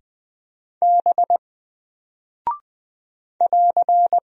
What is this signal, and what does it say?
Morse code, heard as a keyed tone, about 20 wpm, 700 Hz. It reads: B <AR>
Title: Single Character (Letter:Number:Punctuation:Prosign) 20wpm